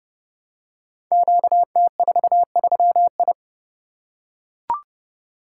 Morse code, heard as a keyed tone, about 30 wpm, 700 Hz. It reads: QT43I
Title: Code Groups - 5 Letter:Numbers - 30wpm